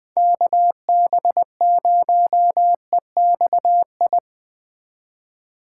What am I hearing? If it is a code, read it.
Morse code, heard as a keyed tone, about 20 wpm, 700 Hz. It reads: KB0EXI
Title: Speed Racing - USA 2x3 Call Signs 17wpm